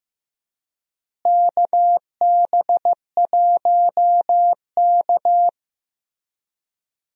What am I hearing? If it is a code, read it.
Morse code, heard as a keyed tone, about 16 wpm, 700 Hz. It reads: KB1K
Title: USA Call Signs - 4 Characters - Repeated 3x - 15wpm